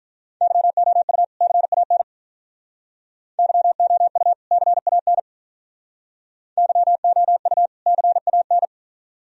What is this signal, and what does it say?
Morse code, heard as a keyed tone, about 46 wpm, 700 Hz. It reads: YOU CAN YOU CAN YOU CAN